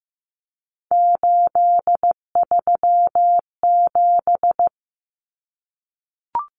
Morse code, heard as a keyed tone, about 15 wpm, 700 Hz. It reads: 837
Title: Numbers - 3-Digit 15wpm